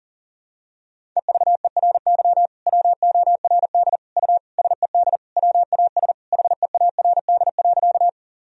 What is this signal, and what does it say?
Morse code, heard as a keyed tone, about 40 wpm, 700 Hz. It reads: EVERY WORD U SED WAS HEARD.
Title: Sentences from Top 400 Words and Abbreviations 40wpm